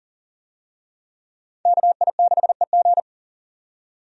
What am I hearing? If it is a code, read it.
Morse code, heard as a keyed tone, about 40 wpm, 700 Hz. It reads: KI6EG
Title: USA Call Signs - 5 Characters - 40wpm